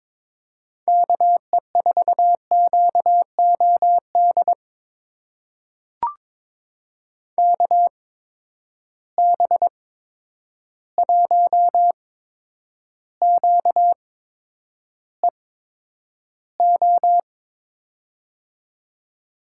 Morse code, heard as a keyed tone, about 22 wpm, spaced as for 9 wpm, 700 Hz. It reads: KE4QOD KB1QEO
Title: USA Call Signs - 6 Characters - 8x Character Spacing - 22wpm